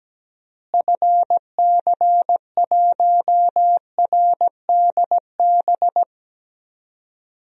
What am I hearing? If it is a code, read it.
Morse code, heard as a keyed tone, about 17 wpm, 700 Hz. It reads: FC1RDB
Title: DXCC-100 Call Signs - 6 Characters - Repeated 3x - 17wpm